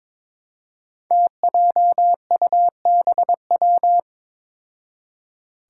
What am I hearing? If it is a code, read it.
Morse code, heard as a keyed tone, about 21 wpm, 700 Hz. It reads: TJUBW
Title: Five Challenging Letter-Number Combinations 22wpm